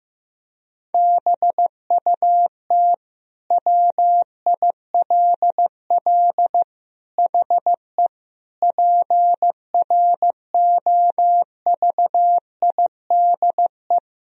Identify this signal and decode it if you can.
Morse code, heard as a keyed tone, about 15 wpm, 700 Hz. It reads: BUT WILL HE PROVIDE
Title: Sentences from Top 1000 Words - Review Part 3 - 15wpm